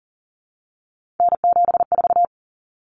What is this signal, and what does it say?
Morse code, heard as a keyed tone, about 40 wpm, 700 Hz. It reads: N74
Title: Common Cut Numbers - 3-Digit 40wpm